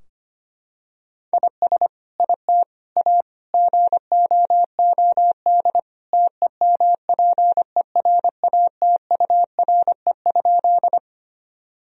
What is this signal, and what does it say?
Morse code, heard as a keyed tone, about 25 wpm, 700 Hz. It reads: IS IT A GOOD TEMPERATURE?
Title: Sentences from Top 1000 Words - Review Part 4 - 25wpm